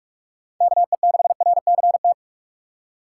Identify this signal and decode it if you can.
Morse code, heard as a keyed tone, about 46 wpm, 700 Hz. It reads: KE6RCT
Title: USA Call Signs - 6 Characters - 4x Character Spacing - 45wpm